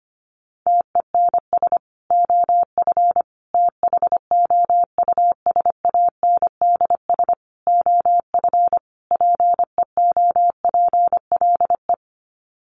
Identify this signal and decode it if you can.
Morse code, heard as a keyed tone, about 25 wpm, 700 Hz. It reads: TENS OF THOUSANDS OF PEOPLE
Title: Sets of 5 Words Encore2 25wpm